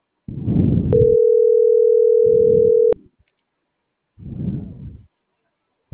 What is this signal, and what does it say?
Ringback tone (North America), 440 + 480 Hz.
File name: CENTRAL MAERSK / out gabrielle.melendez - .28449